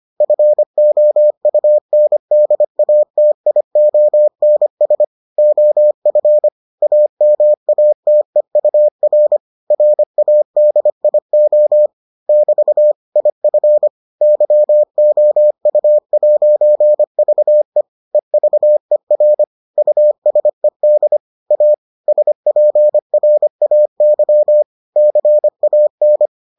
Morse code, heard as a keyed tone, about 25 wpm, 600 Hz. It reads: FOUNDATIONS OF AMATEUR RADIO <BT> IF YOU'VE EVER USED A SPRAY CAN